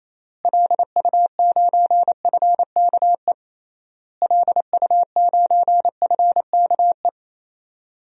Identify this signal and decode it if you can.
Morse code, heard as a keyed tone, about 28 wpm, 700 Hz. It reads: LU9FKE LU9FKE
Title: DXCC-100 Call Signs - 6 Characters - Repeated 3x - 28wpm